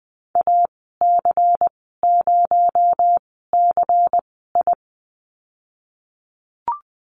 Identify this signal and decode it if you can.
Morse code, heard as a keyed tone, about 20 wpm, 700 Hz. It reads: A C 0 C I
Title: 231 Lesson 37g - Letter x - Call Signs 20wpm